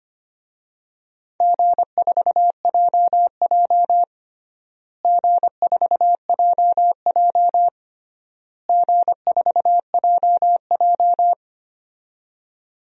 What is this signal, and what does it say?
Morse code, heard as a keyed tone, about 25 wpm, 700 Hz. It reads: G4JJ G4JJ G4JJ
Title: DXCC-100 Call Signs - 4 Characters - Repeated 3x - 25wpm